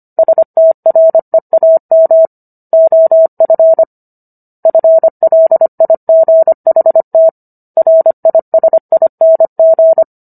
Morse code, heard as a keyed tone, about 25 wpm, 650 Hz. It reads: STREAM OF FLIGHT RISING